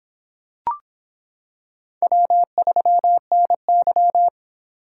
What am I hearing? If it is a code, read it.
Morse code, heard as a keyed tone, about 26 wpm, 700 Hz. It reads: W3NY